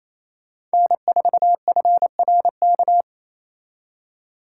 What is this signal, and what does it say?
Morse code, heard as a keyed tone, about 29 wpm, 700 Hz. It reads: N4FRK
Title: Speed Racing - USA 1x3 Call Signs 28wpm